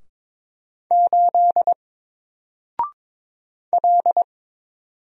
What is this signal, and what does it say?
Morse code, heard as a keyed tone, about 22 wpm, 700 Hz. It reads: 8 L
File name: Single-Letter-Number-22wpm